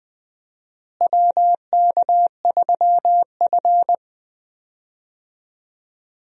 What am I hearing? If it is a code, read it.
Morse code, heard as a keyed tone, about 20 wpm, 700 Hz. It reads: WK3F
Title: USA Call Signs - 4 Characters - Repeated 2x - 20wpm